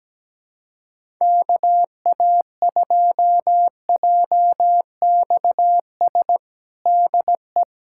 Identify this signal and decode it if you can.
Morse code, heard as a keyed tone, about 17 wpm, 700 Hz. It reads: KA2JXS DE
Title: USA Callsign DE Callsign - 6 Characters - 17wpm